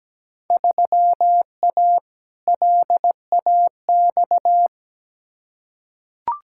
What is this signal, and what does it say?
Morse code, heard as a keyed tone, about 17 wpm, 700 Hz. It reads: 3A LAX
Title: Field Day Exchanges - 17wpm